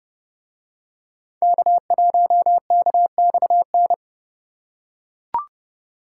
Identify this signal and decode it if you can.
Morse code, heard as a keyed tone, about 28 wpm, 700 Hz. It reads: K1KXN